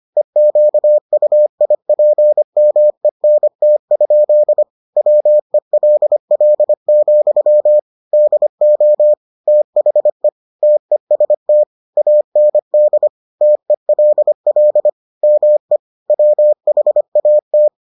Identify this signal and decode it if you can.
Morse code, heard as a keyed tone, about 25 wpm, 600 Hz. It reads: EQUIPMENT? WELL, DO THE TEST AND TELL ME WHAT